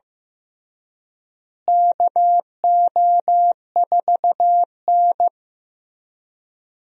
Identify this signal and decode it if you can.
Morse code, heard as a keyed tone, about 15 wpm, 700 Hz. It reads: KO4N
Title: USA Call Signs - 4 Characters - Repeated 3x - 15wpm